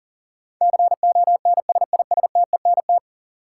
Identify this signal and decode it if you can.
Morse code, heard as a keyed tone, about 38 wpm, 700 Hz. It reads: CONSISTENT
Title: Sentences from Top 1500 Words - New Word Drill Alt - 40wpm